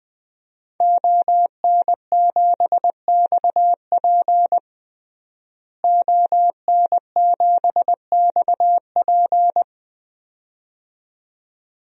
Morse code, heard as a keyed tone, about 20 wpm, 700 Hz. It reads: ON7XP ON7XP
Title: DXCC-100 Call Signs - 5 Characters - Repeated 2x - 20wpm